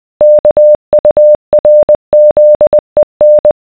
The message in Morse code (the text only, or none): KURZEN